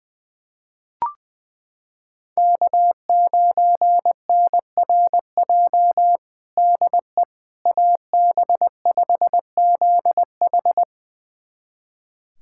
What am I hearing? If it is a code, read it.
Morse code, heard as a keyed tone, about 21 wpm, 700 Hz. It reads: K9NRJ DE AB5ZH
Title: USA Callsign DE Callsign - 5 Characters - 20wpm